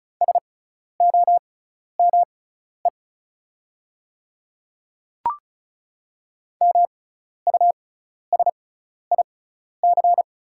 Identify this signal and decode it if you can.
Morse code, heard as a keyed tone, about 34 wpm, spaced as for 12 wpm, 700 Hz. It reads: SOME MUSIC
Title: 6x Character Spacing - Top 300 Words - 35wpm